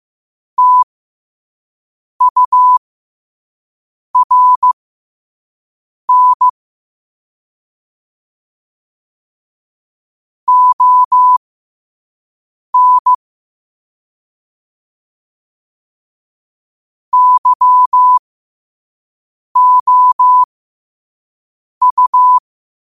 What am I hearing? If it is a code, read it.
Morse code, heard as a keyed tone, about 15 wpm, spaced as for 5 wpm, 1000 Hz. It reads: TURN ON YOU
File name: qotdfiles / 5wpm